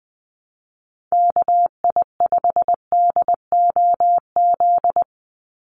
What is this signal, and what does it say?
Morse code, heard as a keyed tone, about 20 wpm, 700 Hz. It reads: KI5DOZ
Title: USA Call Signs 20wpm